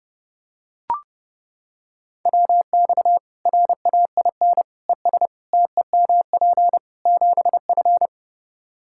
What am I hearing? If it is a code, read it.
Morse code, heard as a keyed tone, about 30 wpm, 700 Hz. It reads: WX RAIN ES TEMP 7F
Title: Practice QSO - Exchange 2 - Info - 30wpm